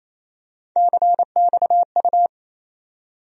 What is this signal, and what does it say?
Morse code, heard as a keyed tone, about 28 wpm, 700 Hz. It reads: CXU
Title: USA Call Sign Postfixes - 3 Letters 28wpm